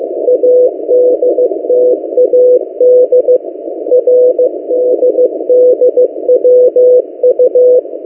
Morse code, heard as a keyed tone, about 15 wpm, 500 Hz. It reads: ADTAD RDDWU
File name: M08a- -1355utc-8096khz